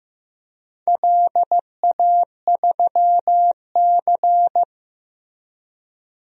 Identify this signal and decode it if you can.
Morse code, heard as a keyed tone, about 15 wpm, 700 Hz. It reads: LA3C
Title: DXCC-100 Call Signs - 4 Characters - Repeated 3x - 15wpm